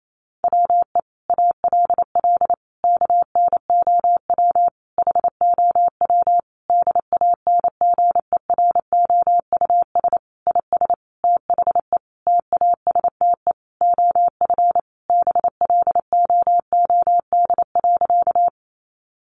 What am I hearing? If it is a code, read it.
Morse code, heard as a keyed tone, about 28 wpm, 700 Hz. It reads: WE ALL KNOW HOW DANGEROUS IS THE TASTE OF BLOOD.
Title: Sentences from Top 1200 Words 28wpm